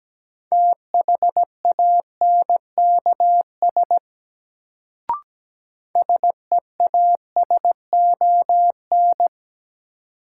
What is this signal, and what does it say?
Morse code, heard as a keyed tone, about 17 wpm, 700 Hz. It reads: THANKS SEASON
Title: Sets of 5 Words - New Word Drill - 17wpm